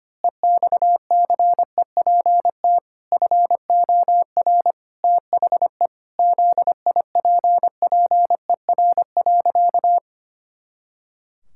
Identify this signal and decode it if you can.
Morse code, heard as a keyed tone, about 25 wpm, 700 Hz. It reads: EXCEPT FOR THE ZIPPER.
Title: Pangram Challenge v2 - 25wpm